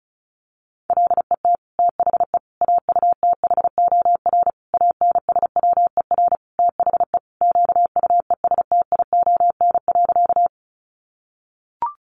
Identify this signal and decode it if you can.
Morse code, heard as a keyed tone, about 34 wpm, 700 Hz. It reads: LET THE AUTHOR ANSWER THE QUESTION.
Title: Sentences from Top 1200 Words 35wpm